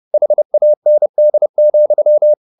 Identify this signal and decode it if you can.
Morse code, heard as a keyed tone, about 30 wpm, 600 Hz. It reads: HAND,